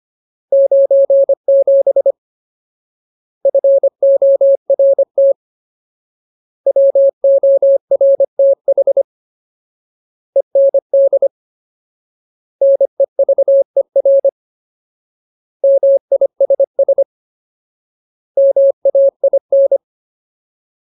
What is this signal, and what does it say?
Morse code, heard as a keyed tone, about 25 wpm, 550 Hz. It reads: 97 FORT WORTH END NEVER MISS MAIN
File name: Ultimate Word List / 25WPM Ultimate